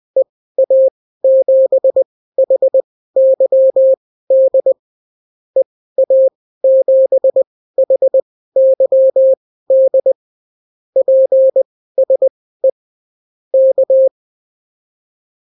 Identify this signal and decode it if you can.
Morse code, heard as a keyed tone, about 20 wpm, spaced as for 15 wpm, 530 Hz. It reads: EA7HYD EA7HYD PSE K
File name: CW QSO libroCW4 20-10ppm